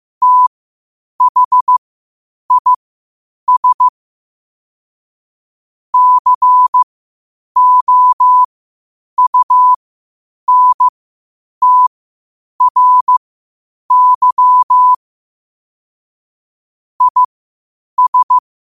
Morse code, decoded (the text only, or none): THIS COUNTRY IS